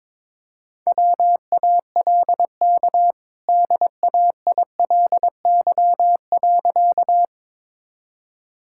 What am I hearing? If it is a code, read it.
Morse code, heard as a keyed tone, about 22 wpm, 700 Hz. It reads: WALK DAILY.
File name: practice / 22wpm / Easy Sentences with Hints - 22wpm